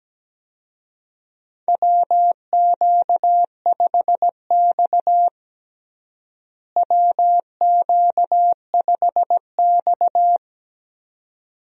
Morse code, heard as a keyed tone, about 17 wpm, 700 Hz. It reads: WQ5X WQ5X
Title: USA Call Signs - 4 Characters - Repeated 2x - 17wpm